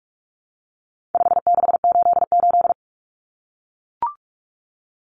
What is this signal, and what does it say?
Morse code, heard as a keyed tone, about 44 wpm, 700 Hz. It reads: 5688
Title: Common Cut Numbers - 4-Digit 45wpm